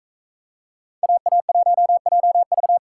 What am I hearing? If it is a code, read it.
Morse code, heard as a keyed tone, about 44 wpm, 700 Hz. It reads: AA1JV